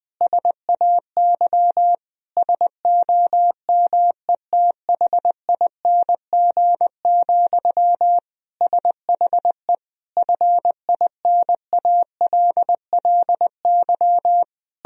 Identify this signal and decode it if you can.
Morse code, heard as a keyed tone, about 21 wpm, 700 Hz. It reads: SAY SOMETHING, SHE FINALLY
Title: Sentences from Top 1000 Words - Review Part 3 - 20wpm